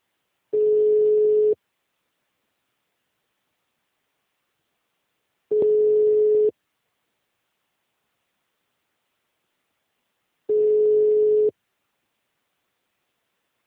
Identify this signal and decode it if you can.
Ringback tone (Europe, 425 Hz).
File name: out- -2004- - - .0